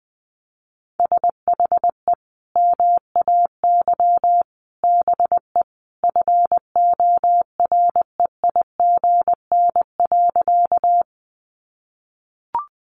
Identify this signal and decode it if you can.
Morse code, heard as a keyed tone, about 20 wpm, 700 Hz. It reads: SHE MAY BE FOREIGN.
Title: Sentences from Top 1400 Words 20wpm